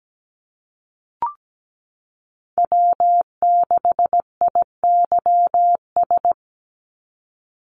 Morse code, heard as a keyed tone, about 17 wpm, 700 Hz. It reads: W6IYS
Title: Call Signs Repeated 2x - 17wpm